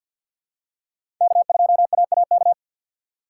Morse code, heard as a keyed tone, about 52 wpm, 700 Hz. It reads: K2AAK
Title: USA Call Signs - 5 Characters - 4x Character Spacing - 50wpm